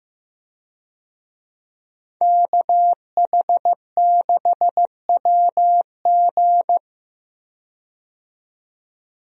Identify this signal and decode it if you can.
Morse code, heard as a keyed tone, about 15 wpm, 700 Hz. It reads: KH6WG
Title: DXCC-100 Call Signs - 5 Characters - 15wpm